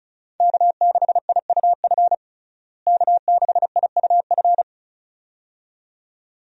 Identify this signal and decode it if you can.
Morse code, heard as a keyed tone, about 36 wpm, 700 Hz. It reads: K6IUF K6IUF
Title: USA Call Signs - 5 Characters - Repeated 2x - 35wpm